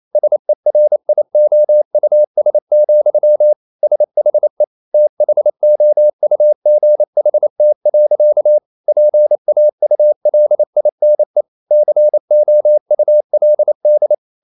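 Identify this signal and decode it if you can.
Morse code, heard as a keyed tone, about 28 wpm, 600 Hz. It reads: SERIOUS, SHE THOUGHT. PAULINE COULD
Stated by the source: Morse code practice at 28 wpm, text of Ken Follett's novel Never in English, from CW med Gnister.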